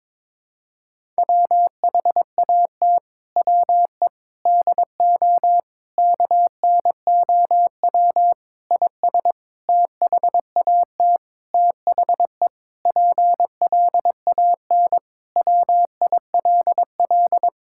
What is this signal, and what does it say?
Morse code, heard as a keyed tone, about 22 wpm, 700 Hz. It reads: WHAT WE DO KNOW IS THAT THE PLAN WILL
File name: Sentences with 7-Word Phrases - 22wpm